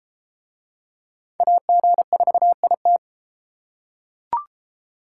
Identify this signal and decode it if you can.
Morse code, heard as a keyed tone, about 32 wpm, 700 Hz. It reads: AG4IT